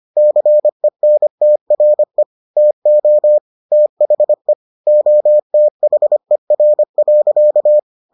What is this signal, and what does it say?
Morse code, heard as a keyed tone, about 25 wpm, 600 Hz. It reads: CENTRE TO THE OTHER.